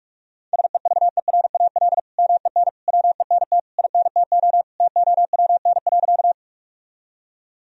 Morse code, heard as a keyed tone, about 44 wpm, 700 Hz. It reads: SEVERAL MEN WENT INTO TOWN.
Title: Sentences from Top 1000 Words - Review Part 2 - 45wpm